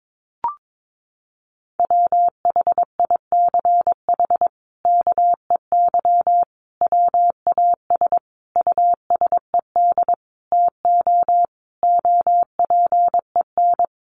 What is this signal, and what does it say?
Morse code, heard as a keyed tone, about 22 wpm, 700 Hz. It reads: WHICH KEY WAS USED TO OPEN
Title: Sentences from Top 600 Words 22wpm